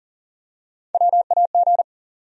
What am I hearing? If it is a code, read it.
Morse code, heard as a keyed tone, about 40 wpm, 700 Hz. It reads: WAG
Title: USA Call Sign Postfixes - 3 Letters 40wpm